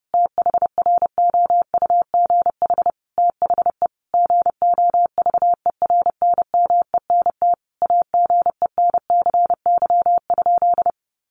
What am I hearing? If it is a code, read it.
Morse code, heard as a keyed tone, about 30 wpm, 700 Hz. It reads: THROUGH THE GOVERNMENT AGENCY?
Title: Sentences from Top 1300 Words 30wpm